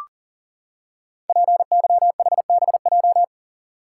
Morse code, heard as a keyed tone, about 40 wpm, 700 Hz. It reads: PYHBJ
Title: Code Groups - 5 Letters - 40wpm